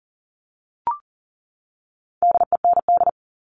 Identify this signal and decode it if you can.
Morse code, heard as a keyed tone, about 40 wpm, 700 Hz. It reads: DEND
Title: All Cut Numbers - 4-Digit 40wpm